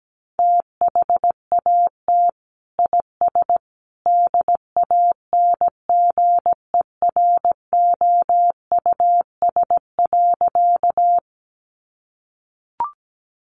Morse code, heard as a keyed tone, about 17 wpm, 700 Hz. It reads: THAT IS DANGEROUS.
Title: Sentences from Top 1200 Words 17wpm